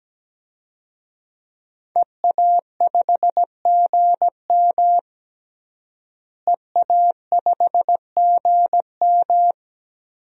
Morse code, heard as a keyed tone, about 17 wpm, 700 Hz. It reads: EA5GM EA5GM
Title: DXCC-100 Call Signs - 5 Characters - Repeated 2x - 17wpm